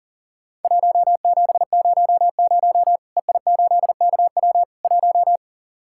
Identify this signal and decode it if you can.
Morse code, heard as a keyed tone, about 44 wpm, 700 Hz. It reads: 1700 EI8KW 1
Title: Worked All Europe DX Contest - QTC Exchange - 40wpm